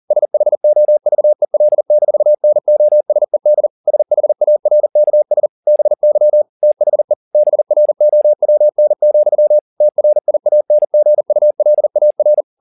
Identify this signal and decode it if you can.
Morse code, heard as a keyed tone, about 40 wpm, 600 Hz. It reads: SHOVEL-NOSED SHARKS BY THE BROWN, TRIANGULAR
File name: the old man and the sea 2 40wpm